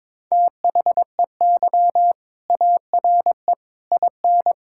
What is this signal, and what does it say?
Morse code, heard as a keyed tone, about 21 wpm, 700 Hz. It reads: THEY ARE IN
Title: Sentences from Top 1000 Words - Review Part 6 - 22wpm